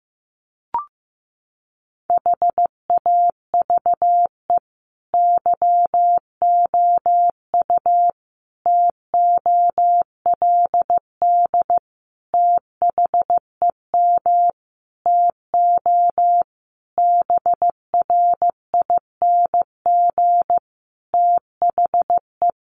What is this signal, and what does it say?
Morse code, heard as a keyed tone, about 14 wpm, 700 Hz. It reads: HAVE YOU TOLD THEM TO BRING THE
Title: Sentences from Top 500 Words 15wpm